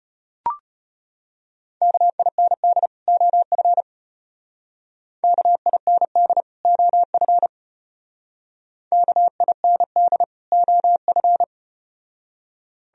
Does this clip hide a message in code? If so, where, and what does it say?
Yes: Morse code from 1.8 to 11.4 s, heard as a keyed tone, about 35 wpm, 700 Hz. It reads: KIND OF KIND OF KIND OF